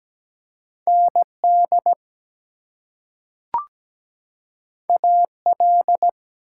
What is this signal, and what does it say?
Morse code, heard as a keyed tone, about 17 wpm, 700 Hz. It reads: ND AL
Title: US States and Canadian Provinces 17wpm